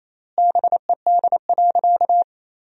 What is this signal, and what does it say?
Morse code, heard as a keyed tone, about 28 wpm, 700 Hz. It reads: BED.
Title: 3x Word-Spacing - Sentences from Top 500 Words - 28wpm